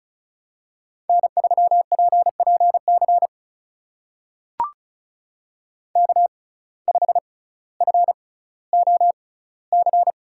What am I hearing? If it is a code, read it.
Morse code, heard as a keyed tone, about 36 wpm, spaced as for 18 wpm, 700 Hz. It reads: N3PPC K5FOC
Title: USA Call Signs - 5 Characters - 6x Character Spacing - 35wpm